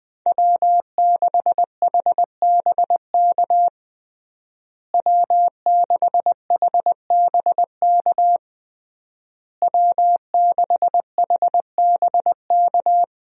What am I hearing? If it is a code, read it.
Morse code, heard as a keyed tone, about 20 wpm, 700 Hz. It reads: W6HBK W6HBK W6HBK
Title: USA Call Signs - 5 Characters - Repeated 3x - 20wpm